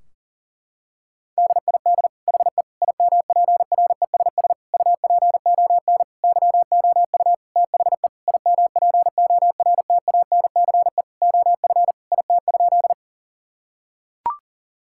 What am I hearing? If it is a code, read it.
Morse code, heard as a keyed tone, about 40 wpm, 700 Hz. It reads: DID HE IMPRESS UPON YOU THE IMPORTANCE OF IT?
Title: Sentences from Top 1600 Words 40wpm